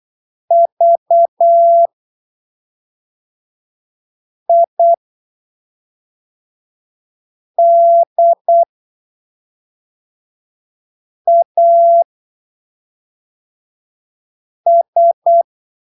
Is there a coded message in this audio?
Yes: Morse code from 0.5 to 15.4 s, heard as a keyed tone, about 8 wpm, spaced as for 3 wpm, 680 Hz. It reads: VIDAS